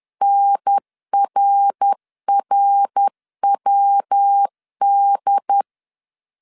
Morse code, heard as a keyed tone, about 10 wpm, 800 Hz. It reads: NRRWD
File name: CW / DET / 1 3 1500C S C 10wpm 3335